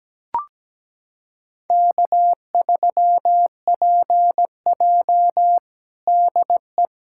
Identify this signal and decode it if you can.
Morse code, heard as a keyed tone, about 17 wpm, 700 Hz. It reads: K3PJ DE
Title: USA Callsign DE Callsign - 4 Characters - 17wpm